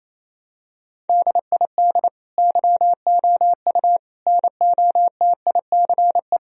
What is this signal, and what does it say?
Morse code, heard as a keyed tone, about 28 wpm, 700 Hz. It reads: DID YOU NOTICE
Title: Sentences from Top 1000 Words - Review Part 2 - 28wpm